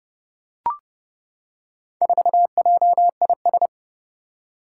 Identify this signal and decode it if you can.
Morse code, heard as a keyed tone, about 28 wpm, 700 Hz. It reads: 4JIS
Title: Four Challenging Letter-Number Combinations 30wpm